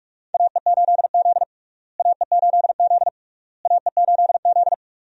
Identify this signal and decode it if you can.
Morse code, heard as a keyed tone, about 46 wpm, 700 Hz. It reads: AE8Z AE8Z AE8Z
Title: USA Call Signs - 4 Characters - Repeated 3x - 45wpm